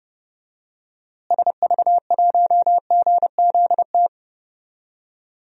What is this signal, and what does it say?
Morse code, heard as a keyed tone, about 30 wpm, 700 Hz. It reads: SV1GZT
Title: DXCC-100 Call Signs - 6 Characters - Repeated 3x - 30wpm